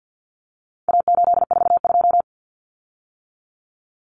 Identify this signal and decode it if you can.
Morse code, heard as a keyed tone, about 50 wpm, 700 Hz. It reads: A842
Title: Common Cut Numbers - 4-Digit 50wpm